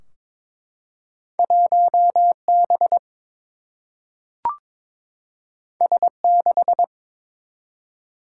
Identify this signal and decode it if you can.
Morse code, heard as a keyed tone, about 22 wpm, 700 Hz. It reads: 1B S6
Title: Single-Letter Single-Number 22wpm